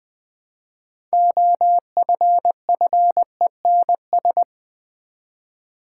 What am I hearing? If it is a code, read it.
Morse code, heard as a keyed tone, about 20 wpm, 700 Hz. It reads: OFFENS
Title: Build A Word - 3 Letters At A Time - 9-Letter Words - 20wpm